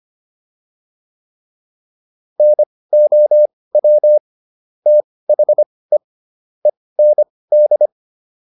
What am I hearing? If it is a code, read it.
Morse code, heard as a keyed tone, about 25 wpm, spaced as for 18 wpm, 600 Hz. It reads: NOW THE END